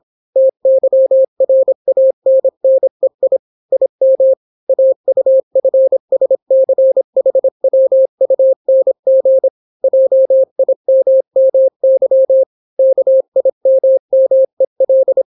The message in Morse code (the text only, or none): TYRANNEI IM AUFSCHWUNG JIMMY KIMMEL